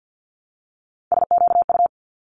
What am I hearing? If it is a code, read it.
Morse code, heard as a keyed tone, about 50 wpm, 700 Hz. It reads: SQU
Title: Multi-letter Phonemes 50wpm